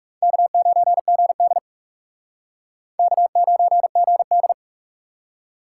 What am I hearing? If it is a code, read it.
Morse code, heard as a keyed tone, about 44 wpm, 700 Hz. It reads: K9GD K9GD